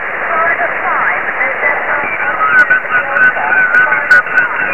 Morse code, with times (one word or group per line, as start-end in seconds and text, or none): none